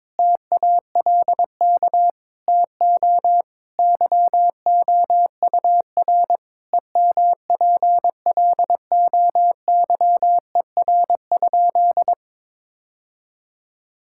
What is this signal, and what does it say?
Morse code, heard as a keyed tone, about 21 wpm, 700 Hz. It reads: TALK TO YOUR EMPLOYER?
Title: Sentences from Top 1600 Words 22wpm